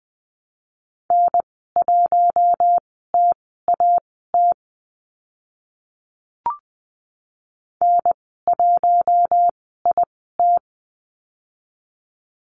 Morse code, heard as a keyed tone, about 20 wpm, spaced as for 15 wpm, 700 Hz. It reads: N1TAT N1IT
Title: 042 Lesson 8g - Number 1 - Call Signs 20wpm